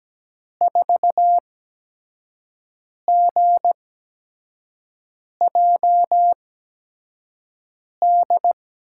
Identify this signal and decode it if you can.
Morse code, heard as a keyed tone, about 17 wpm, 700 Hz. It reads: 4 G J D
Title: USA Call Signs - 6 Characters - 8x Character Spacing - 17wpm